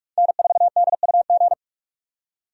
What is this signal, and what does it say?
Morse code, heard as a keyed tone, about 47 wpm, 700 Hz. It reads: N4DUG
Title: USA Call Signs - 5 Characters - 45wpm